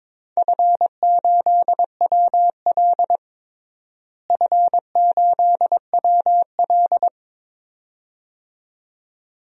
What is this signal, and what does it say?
Morse code, heard as a keyed tone, about 22 wpm, 700 Hz. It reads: F8WL F8WL
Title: DXCC-100 Call Signs - 4 Characters - Repeated 2x - 22wpm